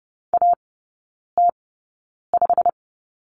Morse code, heard as a keyed tone, about 30 wpm, spaced as for 9 wpm, 690 Hz. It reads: AT5
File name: Lesson 16e - Letter c - Three Characters 30wpm